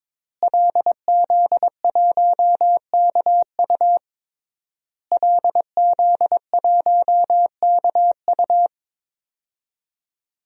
Morse code, heard as a keyed tone, about 22 wpm, 700 Hz. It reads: LZ1KU LZ1KU
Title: DXCC-100 Call Signs - 5 Characters - Repeated 3x - 22wpm